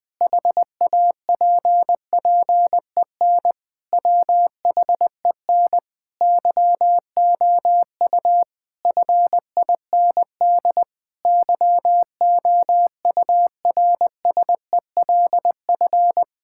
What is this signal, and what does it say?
Morse code, heard as a keyed tone, about 20 wpm, 700 Hz. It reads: HAPPEN WHEN YOU FIND YOURSELF
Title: Sentences from Top 1000 Words - Review Part 5 - 20wpm